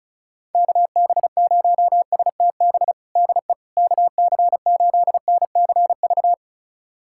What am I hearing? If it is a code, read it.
Morse code, heard as a keyed tone, about 34 wpm, 700 Hz. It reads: KB0STB DE KC8NCV
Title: USA Callsign DE Callsign - 6 Characters - 35wpm